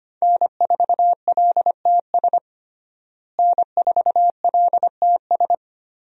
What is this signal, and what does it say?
Morse code, heard as a keyed tone, about 25 wpm, 700 Hz. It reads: N4LTS N4LTS
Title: USA Call Signs - 5 Characters - Repeated 2x - 25wpm